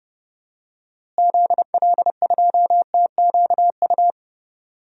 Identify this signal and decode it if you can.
Morse code, heard as a keyed tone, about 32 wpm, 700 Hz. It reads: ZL2TQU
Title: DXCC-100 Call Signs - 6 Characters - Repeated 2x - 30wpm